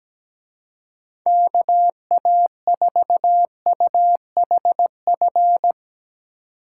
Morse code, heard as a keyed tone, about 18 wpm, 700 Hz. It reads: KA4UHF
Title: Speed Racing - USA 2x3 Call Signs 17wpm